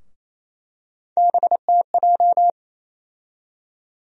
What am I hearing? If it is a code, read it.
Morse code, heard as a keyed tone, about 28 wpm, 700 Hz. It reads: BTJ
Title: USA Call Sign Postfixes - 3 Letters 28wpm